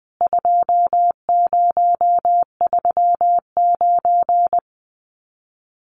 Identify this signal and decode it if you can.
Morse code, heard as a keyed tone, about 20 wpm, 700 Hz. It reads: 2039
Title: CWT Exchange 20wpm